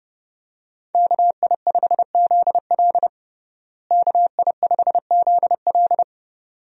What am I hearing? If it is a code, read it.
Morse code, heard as a keyed tone, about 30 wpm, 700 Hz. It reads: KI5ZL KI5ZL
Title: USA Call Signs - 5 Characters - Repeated 2x - 30wpm